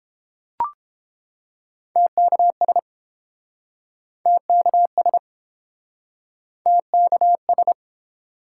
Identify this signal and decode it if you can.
Morse code, heard as a keyed tone, about 30 wpm, 700 Hz. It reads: TKS TKS TKS